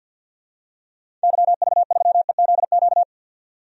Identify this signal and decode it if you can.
Morse code, heard as a keyed tone, about 51 wpm, 700 Hz. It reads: YV3EZQ
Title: DXCC-100 Call Signs - 6 Characters - 50wpm